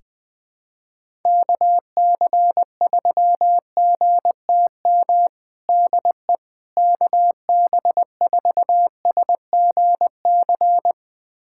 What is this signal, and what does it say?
Morse code, heard as a keyed tone, about 21 wpm, 700 Hz. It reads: KC3GTM DE KB4SGC
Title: USA Callsign DE Callsign - 6 Characters - 20wpm